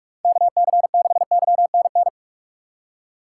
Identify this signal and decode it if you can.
Morse code, heard as a keyed tone, about 46 wpm, 690 Hz. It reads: KC6YNN
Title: USA Call Signs - 6 Characters - 3x Character Spacing - 45wpm